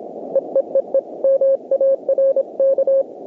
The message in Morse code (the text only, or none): EEEE MARK